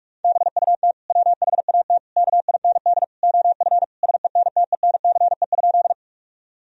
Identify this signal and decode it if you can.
Morse code, heard as a keyed tone, about 41 wpm, 700 Hz. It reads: BUT WHAT KIND OF SENTENCE?
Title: Sentences from Top 1000 Words - Review Part 1 - 45wpm